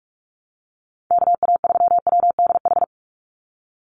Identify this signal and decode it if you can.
Morse code, heard as a keyed tone, about 44 wpm, 700 Hz. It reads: KA3WDH
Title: USA Call Signs 45wpm